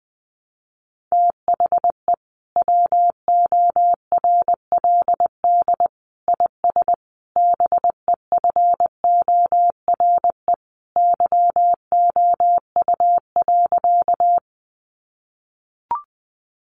Morse code, heard as a keyed tone, about 20 wpm, 700 Hz. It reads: THE WORLD IS BEFORE YOU.
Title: Sentences from Top 200 Words 20wpm